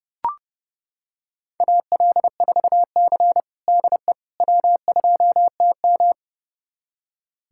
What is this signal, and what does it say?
Morse code, heard as a keyed tone, about 30 wpm, 700 Hz. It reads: AL4C DE W2TM
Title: USA Callsign DE Callsign - 4 Characters - 30wpm